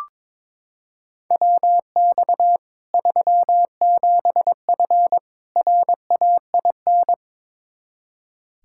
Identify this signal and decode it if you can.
Morse code, heard as a keyed tone, about 22 wpm, 700 Hz. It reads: WX 37F RAIN
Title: Practice QSO - Exchange 2 - Info - 22wpm